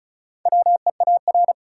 Morse code, heard as a keyed tone, about 36 wpm, 700 Hz. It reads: WEAR
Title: Sentences from Top 1000 Words - Review Part 6 - New Word Drill - 35wpm